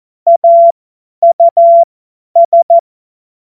Morse code, heard as a keyed tone, about 14 wpm, spaced as for 10 wpm, 680 Hz. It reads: AUS
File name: Morsekurs / Witze / Witz 02. Dezember 2025 50 F70